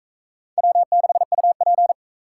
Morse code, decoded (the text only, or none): W6UP